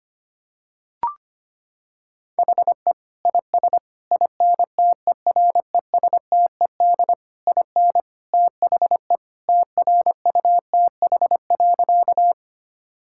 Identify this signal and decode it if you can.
Morse code, heard as a keyed tone, about 25 wpm, 700 Hz. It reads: HE IS INTERESTED IN THE TRUTH.
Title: Sentences from Top 1000 Words - Review Part 6 - 25wpm